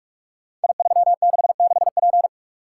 Morse code, heard as a keyed tone, about 44 wpm, 700 Hz. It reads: I366P
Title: Code Groups - 5 Letter:Numbers - 45wpm